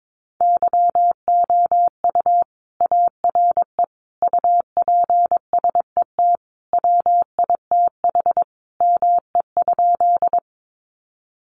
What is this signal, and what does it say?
Morse code, heard as a keyed tone, about 22 wpm, 700 Hz. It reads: YOU ARE UPSET WITH ME?
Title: Sentences from Top 800 Words 22wpm